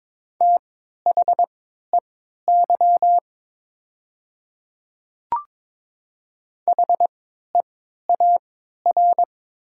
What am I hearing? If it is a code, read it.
Morse code, heard as a keyed tone, about 22 wpm, spaced as for 12 wpm, 700 Hz. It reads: THEY HEAR
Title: 3x Character Spacing - Top 300 Words - 22wpm